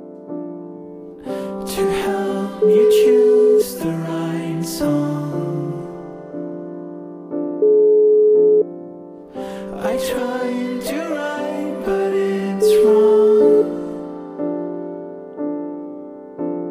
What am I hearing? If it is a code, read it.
Ringback tone (Europe, 425 Hz).